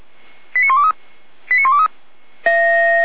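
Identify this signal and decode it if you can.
Five-tone selective call (ZVEI), heard as quick runs of five tones, tones of about 80 ms. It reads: 89123 89123
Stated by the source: ZVEI five-tone selective call received on an SDR, fire brigade alerting.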